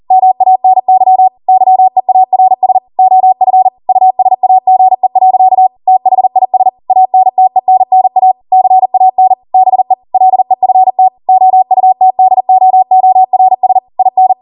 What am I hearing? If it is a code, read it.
Morse code, heard as a keyed tone, about 40 wpm, 750 Hz. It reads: MANY YEARS OF USAGE. THIS ANTENNA CAN BE LEFT OUTDOORS IN
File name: Morse / 40WPM